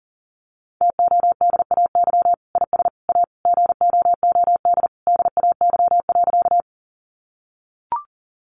Morse code, heard as a keyed tone, about 40 wpm, 700 Hz. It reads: TODAY IS A GOOD DAY.
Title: Sentences from Top 600 Words 40wpm